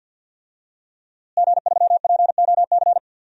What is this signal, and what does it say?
Morse code, heard as a keyed tone, about 47 wpm, 700 Hz. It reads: G3POC
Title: DXCC-100 Call Signs - 5 Characters - Repeated 3x - 50wpm